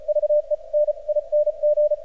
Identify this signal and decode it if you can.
Morse code, heard as a keyed tone, about 36 wpm, spaced as for 30 wpm, 600 Hz. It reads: VENING
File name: archived / CRSnet